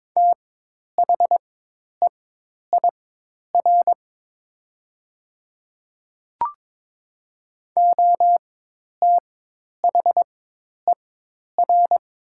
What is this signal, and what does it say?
Morse code, heard as a keyed tone, about 22 wpm, spaced as for 10 wpm, 700 Hz. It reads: THEIR OTHER
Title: 4x Character Spacing - Top 500 Words - 22wpm